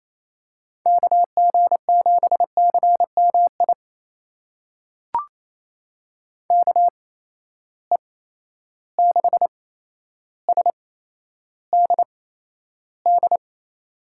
Morse code, heard as a keyed tone, about 27 wpm, spaced as for 12 wpm, 700 Hz. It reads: KG7CMI KE6SDD
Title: USA Call Signs - 6 Characters - 8x Character Spacing - 28wpm